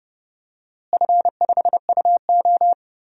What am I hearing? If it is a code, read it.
Morse code, heard as a keyed tone, about 29 wpm, 700 Hz. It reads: F5UO
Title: DXCC-100 Call Signs - 4 Characters - Repeated 2x - 30wpm